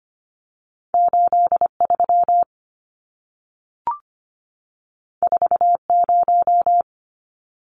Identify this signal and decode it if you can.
Morse code, heard as a keyed tone, about 25 wpm, 700 Hz. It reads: 83 40
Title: Numbers - 2-Digit 25wpm